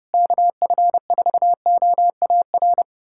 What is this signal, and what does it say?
Morse code, heard as a keyed tone, about 30 wpm, 700 Hz. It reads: KF4OAR
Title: USA Callsign DE Callsign - 6 Characters - 30wpm